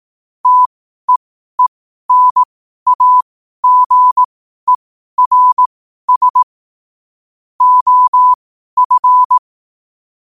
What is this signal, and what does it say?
Morse code, heard as a keyed tone, about 18 wpm, spaced as for 12 wpm, 1000 Hz. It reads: TEENAGERS OF
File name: qotdfiles / 12wpm